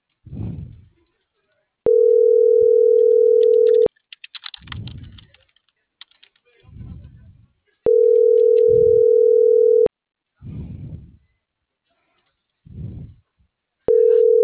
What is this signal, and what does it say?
Ringback tone (North America), 440 + 480 Hz.